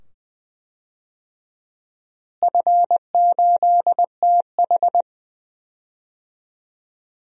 Morse code, heard as a keyed tone, about 20 wpm, 700 Hz. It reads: F8TH
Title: DXCC-100 Call Signs - 4 Characters - Repeated 2x - 20wpm